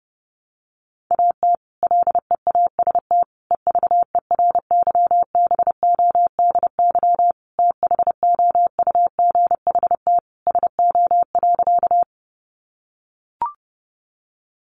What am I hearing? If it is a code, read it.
Morse code, heard as a keyed tone, about 30 wpm, 700 Hz. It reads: AT LEAST EVERYBODY THOUGHT SO.
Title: Sentences from Top 1400 Words 30wpm